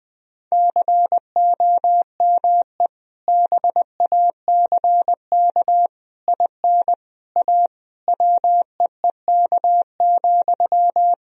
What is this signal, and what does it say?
Morse code, heard as a keyed tone, about 20 wpm, 700 Hz. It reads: COME BACK IN A WEEK,
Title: Sentences from Top 1000 Words - Review Part 3 - 20wpm